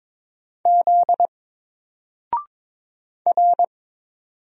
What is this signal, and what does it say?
Morse code, heard as a keyed tone, about 22 wpm, 700 Hz. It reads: Z R
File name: Single-Letter-Number-22wpm